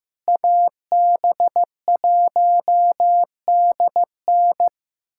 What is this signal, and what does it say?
Morse code, heard as a keyed tone, about 16 wpm, 700 Hz. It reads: AB1DN
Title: USA Callsign DE Callsign - 5 Characters - 15wpm